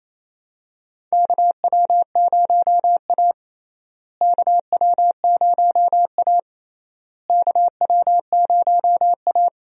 Morse code, heard as a keyed tone, about 28 wpm, 700 Hz. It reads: KW0A KW0A KW0A
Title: USA Call Signs - 4 Characters - Repeated 3x - 28wpm